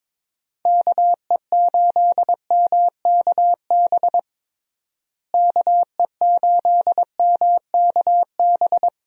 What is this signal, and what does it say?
Morse code, heard as a keyed tone, about 22 wpm, 700 Hz. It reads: KE8MKB KE8MKB
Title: USA Call Signs - 6 Characters - Repeated 2x - 22wpm